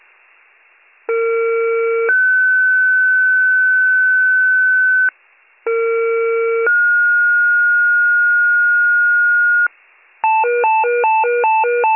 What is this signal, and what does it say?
Two-tone page, heard as a one-second tone, then a three-second tone, about 460 then 1500 Hz, about 460 then 1430 Hz. It is heard from 1.1 to 9.7 s.